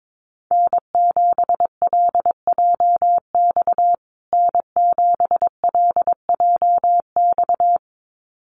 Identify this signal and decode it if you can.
Morse code, heard as a keyed tone, about 22 wpm, 700 Hz. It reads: N7LJX N7LJX
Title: USA Call Signs Repeated 2x 22wpm